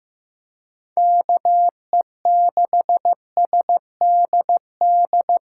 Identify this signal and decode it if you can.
Morse code, heard as a keyed tone, about 15 wpm, 700 Hz. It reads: KE6SDD
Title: USA Call Signs - 6 Characters - 3x Character Spacing - 15wpm